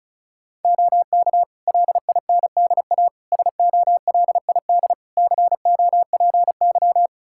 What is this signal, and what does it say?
Morse code, heard as a keyed tone, about 36 wpm, 700 Hz. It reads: OK LINDA SOLID COPY
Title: Practice QSO - Exchange 3 - More Info - 35wpm